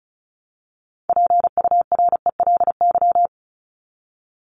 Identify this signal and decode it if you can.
Morse code, heard as a keyed tone, about 34 wpm, 700 Hz. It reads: PURELY
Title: LY Suffix - Top 100 Words with Base Words 35wpm